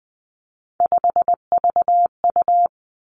Morse code, heard as a keyed tone, about 20 wpm, 700 Hz. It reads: 5VU
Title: Three Challenging Letter-Number Combinations 20wpm